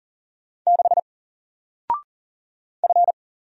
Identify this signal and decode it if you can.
Morse code, heard as a keyed tone, about 42 wpm, 700 Hz. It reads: 6 F
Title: Single Letter-Number Alt 40wpm